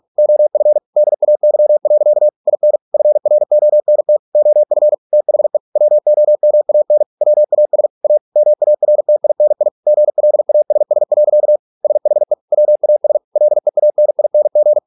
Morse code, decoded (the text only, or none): OF DAY. IN FRONT OF THE WOMAN WAS A MARTINI GLASS. SHE WAS LEANING